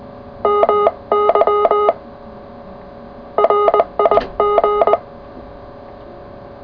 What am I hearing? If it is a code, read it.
Morse code, heard as a keyed tone, about 20 wpm, 420 Hz. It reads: MY RIG
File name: full / cw qso